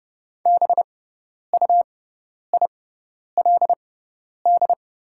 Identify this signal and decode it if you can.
Morse code, heard as a keyed tone, about 30 wpm, spaced as for 10 wpm, 700 Hz. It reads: BUILD
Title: 6x Character Spacing - Top 300 Words - 30wpm